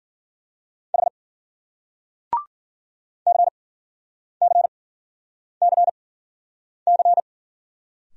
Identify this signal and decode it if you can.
Morse code, heard as a keyed tone, about 51 wpm, spaced as for 9 wpm, 700 Hz. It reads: H CCCC